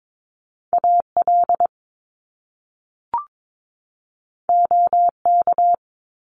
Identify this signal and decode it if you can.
Morse code, heard as a keyed tone, about 22 wpm, 700 Hz. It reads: AL OK
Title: Top Two Letter Combinations in Words 22wpm